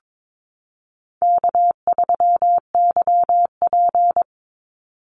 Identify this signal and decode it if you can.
Morse code, heard as a keyed tone, about 22 wpm, 700 Hz. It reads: K3YP
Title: Easy Call Signs - 22wpm